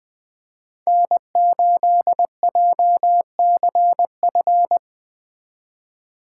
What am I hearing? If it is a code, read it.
Morse code, heard as a keyed tone, about 20 wpm, 700 Hz. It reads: N8JCF
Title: Speed Racing - USA 1x3 Call Signs 17wpm